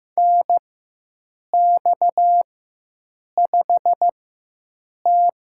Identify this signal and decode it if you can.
Morse code, heard as a keyed tone, about 15 wpm, spaced as for 7 wpm, 700 Hz. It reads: NX5T
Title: USA Call Signs - 4 Characters - 4x Character Spacing - 15wpm